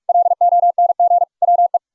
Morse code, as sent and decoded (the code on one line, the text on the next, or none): .-.. --- -. --. / .-- .
LONG WE